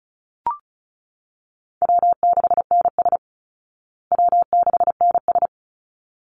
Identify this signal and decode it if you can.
Morse code, heard as a keyed tone, about 34 wpm, 700 Hz. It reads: W6NS W6NS
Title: Call Signs Repeated 3x - 35wpm